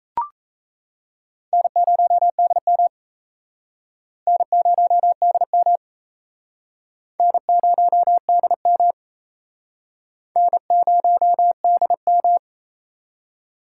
Morse code, heard as a keyed tone, about 37 wpm, 700 Hz. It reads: N0DM N0DM N0DM N0DM